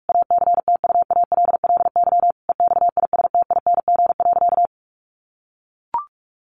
Morse code, heard as a keyed tone, about 44 wpm, 700 Hz. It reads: ACTUALLY EXISTING.
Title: Sentences from Top 1500 Words 45wpm